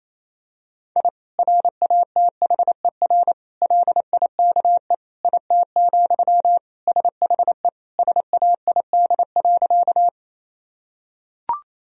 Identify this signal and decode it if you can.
Morse code, heard as a keyed tone, about 27 wpm, 700 Hz. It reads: I RATHER LIKE IT, SHE SAID.
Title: Sentences from Top 1000 Words - Review Part 4 - 28wpm